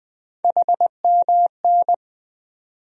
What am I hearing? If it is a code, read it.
Morse code, heard as a keyed tone, about 20 wpm, 700 Hz. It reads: HMN
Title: USA Call Sign Postfixes - 3 Letters 20wpm